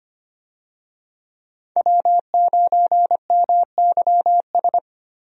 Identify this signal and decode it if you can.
Morse code, heard as a keyed tone, about 25 wpm, 690 Hz. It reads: W9MYS
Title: USA Call Signs - 5 Characters - Repeated 2x - 25wpm